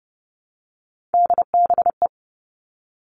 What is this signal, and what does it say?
Morse code, heard as a keyed tone, about 30 wpm, 700 Hz. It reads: DBE
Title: All Cut Numbers - 3-Digit 30wpm